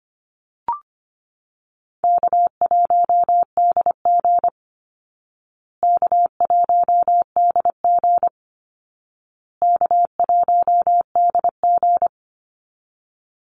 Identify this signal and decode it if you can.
Morse code, heard as a keyed tone, about 25 wpm, 700 Hz. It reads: K1DG K1DG K1DG
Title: Call Signs Repeated 2x - 25wpm